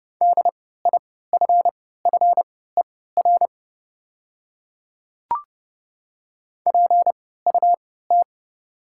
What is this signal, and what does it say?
Morse code, heard as a keyed tone, about 30 wpm, spaced as for 17 wpm, 690 Hz. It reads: DIFFER PUT
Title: 3x Character Spacing - Top 200 Words - 30wpm